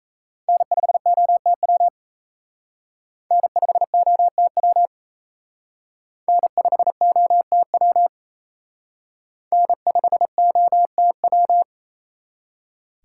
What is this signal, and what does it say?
Morse code, heard as a keyed tone, about 35 wpm, 700 Hz. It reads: N5OTW N5OTW N5OTW N5OTW